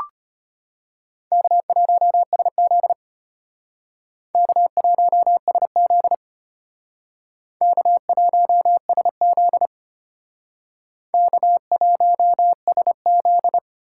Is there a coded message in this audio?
Yes: Morse code from 1.3 to 13.6 s, heard as a keyed tone, about 32 wpm, 700 Hz. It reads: K1SZ K1SZ K1SZ K1SZ